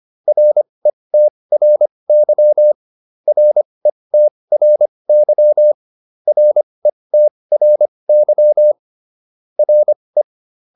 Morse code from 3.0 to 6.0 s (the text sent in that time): RETRY